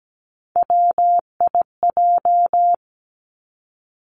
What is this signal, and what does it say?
Morse code, heard as a keyed tone, about 17 wpm, 700 Hz. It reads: WIJ
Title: Three Challenging Letter-Number Combinations 17wpm